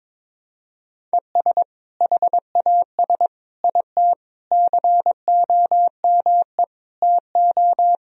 Morse code, heard as a keyed tone, about 22 wpm, 700 Hz. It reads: ES HAS IT COME TO
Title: Sentences from Top 100 Words and Abbreviations 22wpm